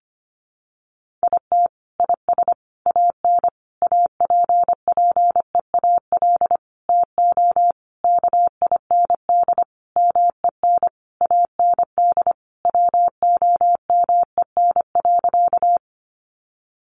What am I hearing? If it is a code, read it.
Morse code, heard as a keyed tone, about 25 wpm, 700 Hz. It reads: IT IS AN APPEAL TO KIND MEN AND WOMEN.
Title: Sentences from Top 1300 Words 25wpm